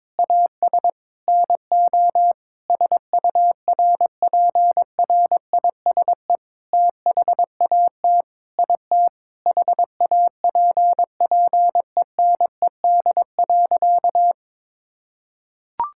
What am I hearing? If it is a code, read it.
Morse code, heard as a keyed tone, about 22 wpm, 700 Hz. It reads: AS NO SURPRISE THAT IT HAPPENED.
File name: practice / 22wpm / Sentences with 8-Word Phrases - 22wpm